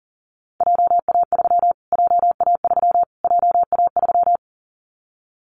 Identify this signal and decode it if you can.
Morse code, heard as a keyed tone, about 40 wpm, 700 Hz. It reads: JA3 JA3 JA3
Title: Callsign Prefixes for Top 25 DXCC - Repeated 3x - 40wpm